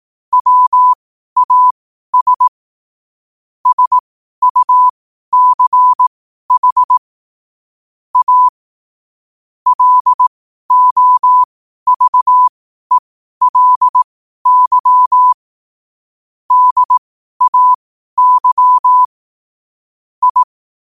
Morse code, heard as a keyed tone, about 18 wpm, spaced as for 12 wpm, 1000 Hz. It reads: WAS SUCH A LOVELY DAY I
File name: qotdfiles / 12wpm